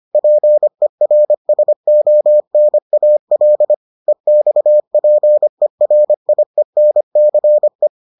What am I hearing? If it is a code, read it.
Morse code, heard as a keyed tone, about 25 wpm, 600 Hz. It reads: PERSONAL EXPERIENCE